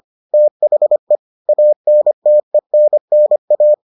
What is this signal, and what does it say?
Morse code, heard as a keyed tone, about 25 wpm, 600 Hz. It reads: THE ANTENNA